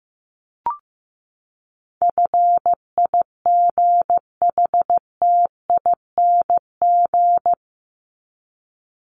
Morse code, heard as a keyed tone, about 15 wpm, 700 Hz. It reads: FIGHTING
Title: ING Suffix - Top 100 Words with Base Words 15wpm